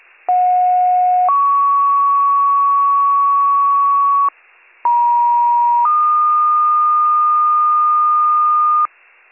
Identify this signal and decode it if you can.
Two-tone page, heard as a one-second tone, then a three-second tone, about 730 then 1120 Hz, about 950 then 1220 Hz, with noise. It is heard from 0.3 to 8.9 s.